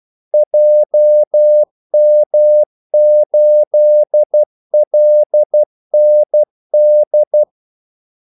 Morse code, heard as a keyed tone, about 12 wpm, 600 Hz. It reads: JM8LND
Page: 2 ekstra ordmellemrum mellem kaldesignalerne